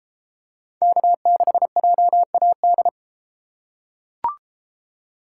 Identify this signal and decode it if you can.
Morse code, heard as a keyed tone, about 31 wpm, 700 Hz. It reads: K6JAD